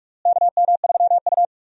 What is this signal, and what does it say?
Morse code, heard as a keyed tone, about 44 wpm, 700 Hz. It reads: KM3U
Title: USA Call Signs - 4 Characters - Repeated 2x - 45wpm